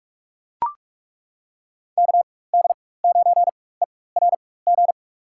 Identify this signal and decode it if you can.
Morse code, heard as a keyed tone, about 42 wpm, spaced as for 21 wpm, 700 Hz. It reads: KD9ERG
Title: USA Call Signs - 6 Characters - 4x Character Spacing - 45wpm